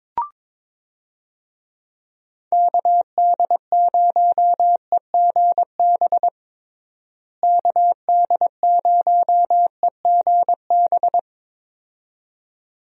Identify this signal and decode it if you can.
Morse code, heard as a keyed tone, about 22 wpm, 700 Hz. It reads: KD0EGB KD0EGB
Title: USA Call Signs - 6 Characters - Repeated 2x - 22wpm